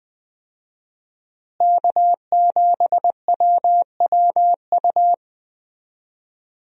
Morse code, heard as a keyed tone, about 20 wpm, 700 Hz. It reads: K7WWU
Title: USA Call Signs - 5 Characters - Repeated 2x - 20wpm